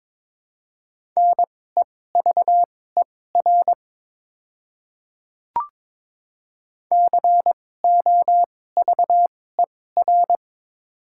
Morse code, heard as a keyed tone, about 22 wpm, spaced as for 16 wpm, 700 Hz. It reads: NEVER COVER
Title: 2x Character Spacing - Top 500 Words - 22wpm